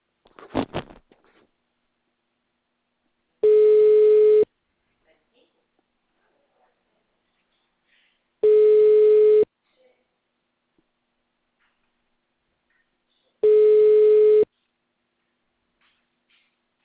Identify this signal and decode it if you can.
Ringback tone (Europe, 425 Hz), with speech.
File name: out- -2009- - - .8